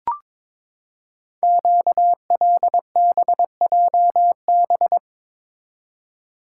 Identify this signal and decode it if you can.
Morse code, heard as a keyed tone, about 22 wpm, 700 Hz. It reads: QLBJB
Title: Code Groups - 5 Letters - 22wpm